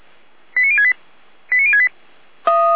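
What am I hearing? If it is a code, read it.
Five-tone selective call (ZVEI), heard as quick runs of five tones, tones of about 70 ms. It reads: 89068 89068
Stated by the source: ZVEI five-tone selective call received on an SDR.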